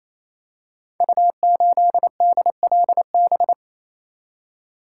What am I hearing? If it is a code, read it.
Morse code, heard as a keyed tone, about 28 wpm, 700 Hz. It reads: U8DLB
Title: Code Groups - 5 Letter:Numbers - 28wpm